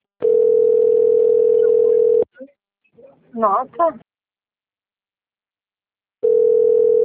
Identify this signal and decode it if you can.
Ringback tone (North America), 440 + 480 Hz, with speech.
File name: rg-1600-unknown- - - .49088